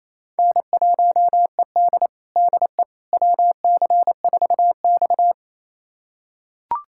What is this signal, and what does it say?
Morse code, heard as a keyed tone, about 27 wpm, 700 Hz. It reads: N1ED DE WC4X
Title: USA Callsign DE Callsign - 4 Characters - 28wpm